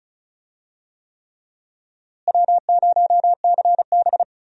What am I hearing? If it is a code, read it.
Morse code, heard as a keyed tone, about 35 wpm, 700 Hz. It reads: W0CB
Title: USA Call Signs - 4 Characters - Repeated 3x - 35wpm